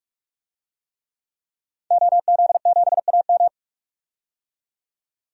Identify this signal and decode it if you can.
Morse code, heard as a keyed tone, about 46 wpm, 700 Hz. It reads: OZ7AM
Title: DXCC-100 Call Signs - 5 Characters - 45wpm